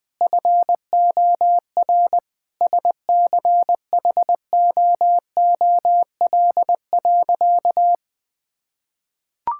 Morse code, heard as a keyed tone, about 20 wpm, 700 Hz. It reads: FOR SCHOOL.
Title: Sentences from Top 1000 Words - Review Part 6 - 20wpm